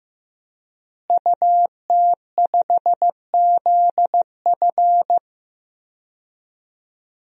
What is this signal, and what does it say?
Morse code, heard as a keyed tone, about 15 wpm, 700 Hz. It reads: UT5ZF
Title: DXCC-100 Call Signs - 5 Characters - Repeated 3x - 15wpm